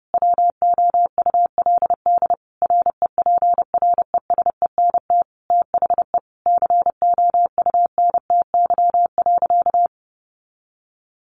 Morse code, heard as a keyed tone, about 30 wpm, 700 Hz. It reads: WOULD REPRESENT THE COUNTY.
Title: Sentences from Top 1200 Words 30wpm